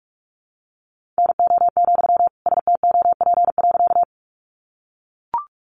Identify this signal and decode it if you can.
Morse code, heard as a keyed tone, about 44 wpm, 700 Hz. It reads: NO, STOP.
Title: Sentences from Top 300 Words 45wpm